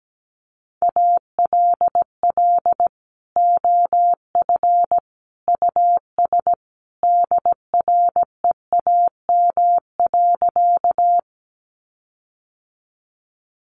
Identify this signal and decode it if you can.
Morse code, heard as a keyed tone, about 17 wpm, 700 Hz. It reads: ALL OF US DREAM.
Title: Sentences from Top 1500 Words 17wpm